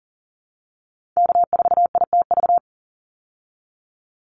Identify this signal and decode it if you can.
Morse code, heard as a keyed tone, about 40 wpm, 700 Hz. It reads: K4ITV
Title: Call Signs Repeated 3x - 40wpm